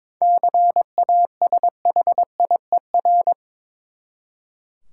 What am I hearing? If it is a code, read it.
Morse code, heard as a keyed tone, about 23 wpm, 700 Hz. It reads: CASHIER
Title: Practice QSO - Exchange 3 - More Info - 22wpm